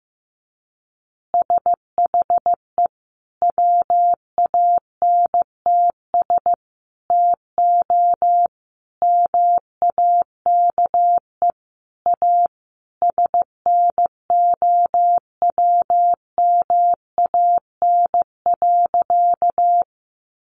Morse code, heard as a keyed tone, about 15 wpm, 700 Hz. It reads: SHE WANTS TO MAKE A SNOWMAN.
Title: Easy Phrases 15wpm